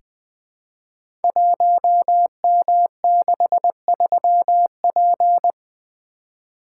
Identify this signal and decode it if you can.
Morse code, heard as a keyed tone, about 20 wpm, 700 Hz. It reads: 1M63P
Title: Code Groups - 5 Letter:Numbers - 20wpm